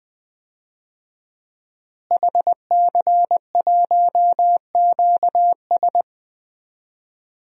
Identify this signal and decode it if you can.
Morse code, heard as a keyed tone, about 20 wpm, 700 Hz. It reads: HC1QS
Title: DXCC-100 Call Signs - 5 Characters - Repeated 3x - 20wpm